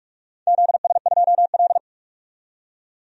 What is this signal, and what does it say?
Morse code, heard as a keyed tone, about 44 wpm, 700 Hz. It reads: ZS2L
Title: DXCC-100 Call Signs - 4 Characters - 45wpm